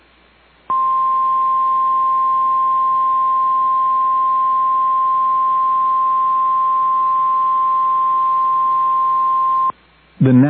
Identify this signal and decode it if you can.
Emergency Alert System (SAME) tones (the 1050 Hz attention tone), with speech. It is heard from 0.7 to 9.8 s.